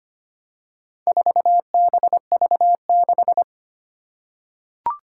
Morse code, heard as a keyed tone, about 25 wpm, 700 Hz. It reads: 4BV6
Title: Four Challenging Letter-Number Combinations 25wpm